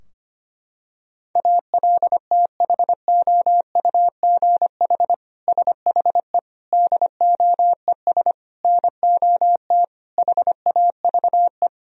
Morse code, heard as a keyed tone, about 25 wpm, 700 Hz. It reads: ALTHOUGH SHE DOES NOT HAVE
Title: Sentences from Top 1000 Words - Review Part 3 - 25wpm